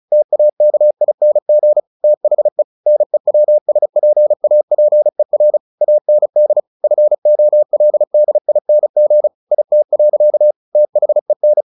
Morse code, heard as a keyed tone, about 34 wpm, 600 Hz. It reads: TAKING THE NEWSPAPER AND FOLDING IT. THEN
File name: the old man and the sea / the old man and the sea 1 35wpm